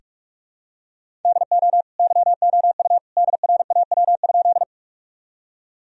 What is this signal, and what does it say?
Morse code, heard as a keyed tone, about 44 wpm, 700 Hz. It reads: DO YOU DRAW?
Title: Sentences from Top 1000 Words - Review Part 1 - 45wpm